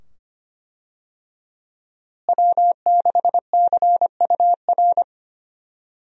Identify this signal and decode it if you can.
Morse code, heard as a keyed tone, about 25 wpm, 700 Hz. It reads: W6CUR
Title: USA Call Signs - 5 Characters - Repeated 2x - 25wpm